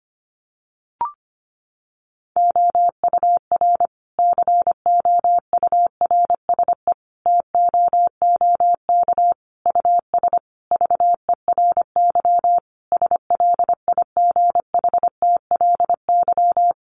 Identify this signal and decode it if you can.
Morse code, heard as a keyed tone, about 25 wpm, 700 Hz. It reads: OUR COURSE TOOK US VERY SLIGHTLY
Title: Sentences from Top 1300 Words 25wpm